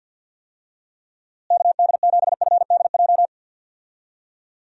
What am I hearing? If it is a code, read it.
Morse code, heard as a keyed tone, about 52 wpm, 700 Hz. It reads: KD7FDJ
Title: USA Call Signs - 6 Characters - Repeated 3x - 50wpm